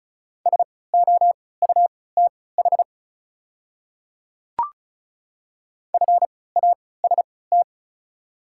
Morse code, heard as a keyed tone, about 36 wpm, spaced as for 20 wpm, 690 Hz. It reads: SOUTH FAST
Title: 3x Character Spacing - Top 500 Words - 35wpm